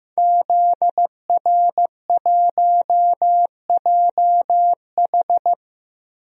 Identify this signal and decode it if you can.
Morse code, heard as a keyed tone, about 16 wpm, 700 Hz. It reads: ZR1JH
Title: DXCC-100 Call Signs - 5 Characters - Repeated 2x - 15wpm